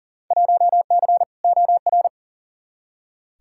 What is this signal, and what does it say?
Morse code, heard as a keyed tone, about 40 wpm, 700 Hz. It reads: 1C OR
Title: Field Day Exchanges - 40wpm